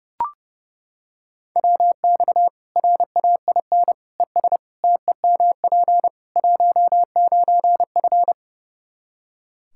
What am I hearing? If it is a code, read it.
Morse code, heard as a keyed tone, about 30 wpm, 700 Hz. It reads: WX RAIN ES TEMP 19F
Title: Practice QSO - Exchange 2 - Info - 30wpm